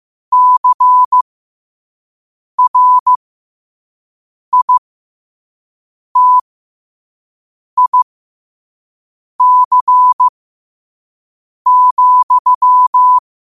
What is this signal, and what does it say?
Morse code, heard as a keyed tone, about 15 wpm, spaced as for 5 wpm, 1000 Hz. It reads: CRITIC,
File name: qotdfiles / 5wpm